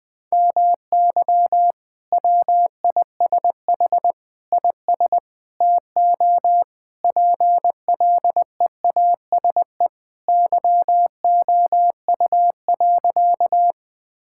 Morse code, heard as a keyed tone, about 21 wpm, 700 Hz. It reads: MY WISH IS TO PLEASE YOU.
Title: Sentences from Top 1600 Words 20wpm